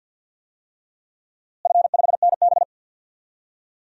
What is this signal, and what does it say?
Morse code, heard as a keyed tone, about 53 wpm, 700 Hz. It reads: F5NB
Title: DXCC-100 Call Signs - 4 Characters - Repeated 3x - 50wpm